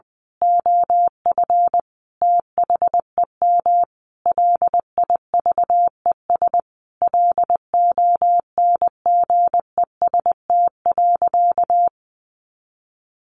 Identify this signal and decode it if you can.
Morse code, heard as a keyed tone, about 20 wpm, 700 Hz. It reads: OF THEM LIVES LONGEST.
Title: A Sound Of Thunder 20wpm